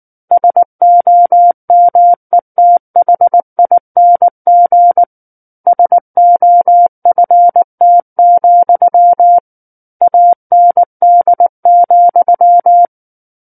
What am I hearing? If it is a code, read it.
Morse code, heard as a keyed tone, about 19 wpm, 700 Hz. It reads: SOMETHING SOFT, AND,
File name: Book One / 10-Chapter 10 - 19 WPM